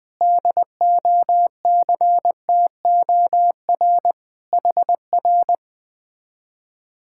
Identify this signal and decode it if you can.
Morse code, heard as a keyed tone, about 20 wpm, 700 Hz. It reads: DOCTOR HR
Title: Speed-Racing Common QSO Phrases - 17wpm